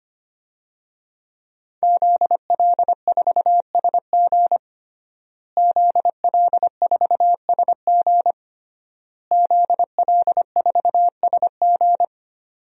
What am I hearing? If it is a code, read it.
Morse code, heard as a keyed tone, about 25 wpm, 700 Hz. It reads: ZL4SG ZL4SG ZL4SG
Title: DXCC-100 Call Signs - 5 Characters - Repeated 3x - 25wpm